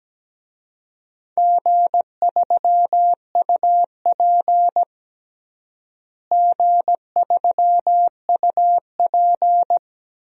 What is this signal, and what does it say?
Morse code, heard as a keyed tone, about 17 wpm, 700 Hz. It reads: G3UP G3UP
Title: DXCC-100 Call Signs - 4 Characters - Repeated 2x - 17wpm